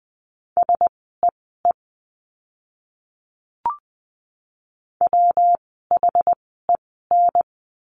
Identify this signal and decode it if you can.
Morse code, heard as a keyed tone, about 20 wpm, spaced as for 15 wpm, 700 Hz. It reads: SEE WHEN
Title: 192 Lesson 31f - Letter v - Words 20wpm